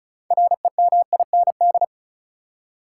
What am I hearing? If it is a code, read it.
Morse code, heard as a keyed tone, about 36 wpm, 700 Hz. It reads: REMIND
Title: Six Letter Words - 35wpm